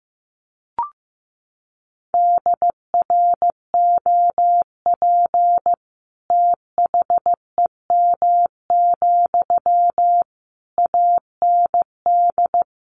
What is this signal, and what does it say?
Morse code, heard as a keyed tone, about 15 wpm, 700 Hz. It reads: DROP THEM, AND
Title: Sentences from Top 1400 Words 15wpm